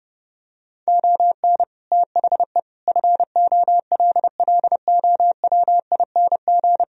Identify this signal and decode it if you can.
Morse code, heard as a keyed tone, about 31 wpm, 700 Hz. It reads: ON THE FOLLOWING
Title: Sentences from Top 1000 Words - Review Part 4 - 30wpm